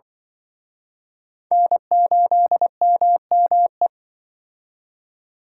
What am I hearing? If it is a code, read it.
Morse code, heard as a keyed tone, about 24 wpm, 700 Hz. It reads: N8MME
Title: Speed Racing - USA 1x3 Call Signs 20wpm